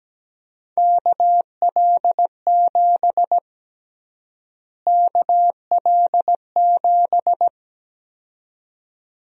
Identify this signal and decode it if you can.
Morse code, heard as a keyed tone, about 17 wpm, 700 Hz. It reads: KL7 KL7
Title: USA Call Sign Prefixes - Repeated 2x - 17wpm